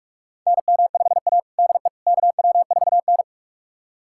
Morse code, heard as a keyed tone, about 41 wpm, 700 Hz. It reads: NM5A DE KW4N
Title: USA Callsign DE Callsign - 4 Characters - 45wpm